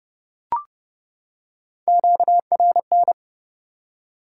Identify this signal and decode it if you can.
Morse code, heard as a keyed tone, about 30 wpm, 700 Hz. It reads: QRN
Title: Top 90 QSO Elements 30wpm